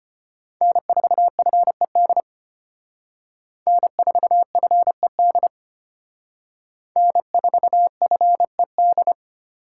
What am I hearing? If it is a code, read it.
Morse code, heard as a keyed tone, about 32 wpm, 700 Hz. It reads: N4FED N4FED N4FED